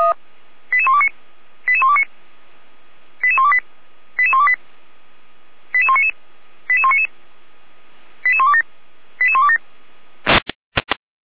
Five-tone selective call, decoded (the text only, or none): 88129 88129 88128 88128 88199 88199 88127 88127